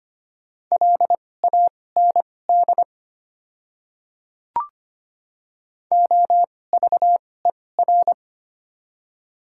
Morse code, heard as a keyed tone, about 25 wpm, spaced as for 18 wpm, 700 Hz. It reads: LAND OVER
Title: 2x Character Spacing - Top 300 Words - 25wpm